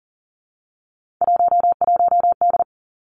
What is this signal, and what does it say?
Morse code, heard as a keyed tone, about 38 wpm, 700 Hz. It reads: 11D
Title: Three Challenging Letter-Number Combinations 40wpm